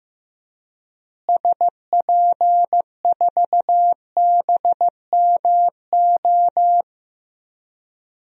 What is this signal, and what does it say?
Morse code, heard as a keyed tone, about 15 wpm, 700 Hz. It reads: SP4BMO
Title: DXCC-100 Call Signs - 6 Characters - Repeated 3x - 15wpm